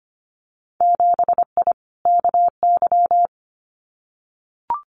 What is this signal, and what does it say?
Morse code, heard as a keyed tone, about 25 wpm, 700 Hz. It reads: 7I KY
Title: Winter Field Day Exchanges Encore 25wpm